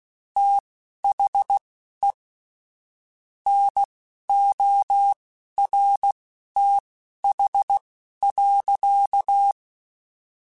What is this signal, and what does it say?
Morse code, heard as a keyed tone, about 16 wpm, spaced as for 11 wpm, 800 Hz. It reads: THE NORTH.
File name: Weather Sets / Weather Sets-mp3s / Weather 4 28 @10